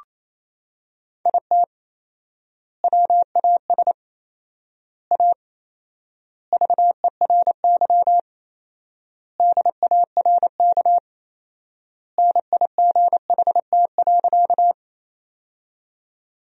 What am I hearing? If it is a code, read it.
Morse code, heard as a keyed tone, about 28 wpm, 700 Hz. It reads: IT WAS A VERY DARK NIGHT.
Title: 4x Word-Spacing - Sentences from Top 500 Words - 28wpm